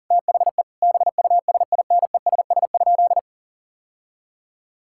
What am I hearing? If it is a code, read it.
Morse code, heard as a keyed tone, about 40 wpm, 700 Hz. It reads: THE BUSINESS?
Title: Sentences from Top 1000 Words - Review Part 5 - 40wpm